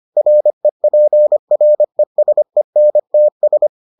Morse code, heard as a keyed tone, about 25 wpm, 600 Hz. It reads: REPRESENTS